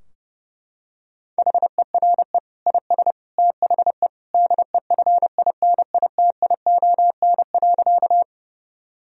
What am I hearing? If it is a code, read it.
Morse code, heard as a keyed tone, about 30 wpm, 700 Hz. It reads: HERE IS THE DEFINITION.
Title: Sentences from Top 1000 Words - Review Part 4 - 30wpm